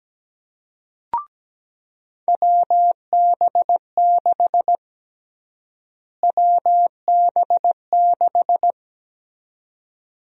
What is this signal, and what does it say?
Morse code, heard as a keyed tone, about 17 wpm, 700 Hz. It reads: WB6 WB6
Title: USA Call Sign Prefixes - Repeated 2x - 17wpm